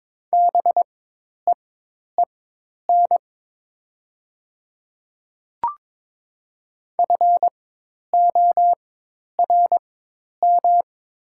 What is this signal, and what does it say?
Morse code, heard as a keyed tone, about 23 wpm, spaced as for 10 wpm, 700 Hz. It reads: BEEN FORM
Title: 4x Character Spacing - Top 300 Words - 22wpm